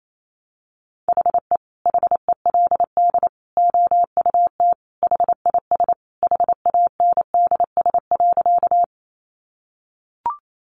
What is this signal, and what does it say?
Morse code, heard as a keyed tone, about 28 wpm, 700 Hz. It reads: HE HELD OUT HIS HANDS.
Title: A Sound Of Thunder 28wpm